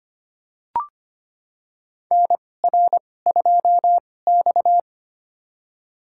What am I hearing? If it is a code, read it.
Morse code, heard as a keyed tone, about 25 wpm, spaced as for 18 wpm, 700 Hz. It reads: NR2X
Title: USA Call Signs - 4 Characters - 2x Character Spacing - 25wpm